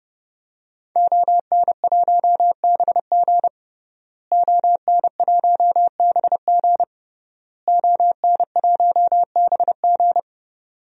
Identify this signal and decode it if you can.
Morse code, heard as a keyed tone, about 31 wpm, 700 Hz. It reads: ON1BG ON1BG ON1BG
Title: DXCC-100 Call Signs - 5 Characters - Repeated 3x - 30wpm